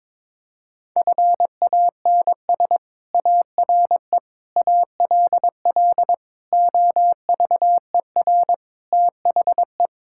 Morse code, heard as a keyed tone, about 22 wpm, 700 Hz. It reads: FANS ARE ALL OVER THE
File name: practice / 22wpm / Sentences with 4-Word Phrases - 22wpm